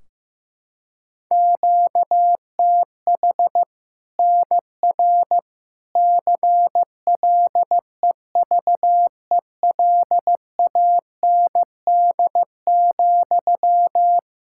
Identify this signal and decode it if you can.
Morse code, heard as a keyed tone, about 15 wpm, 700 Hz. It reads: QTH NR CLEVELAND,
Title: K1USN SST QSO Practice 15wpm